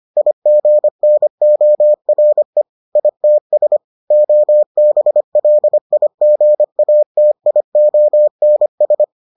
Morse code, heard as a keyed tone, about 25 wpm, 600 Hz. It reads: IGNORE ITS OBLIGATIONS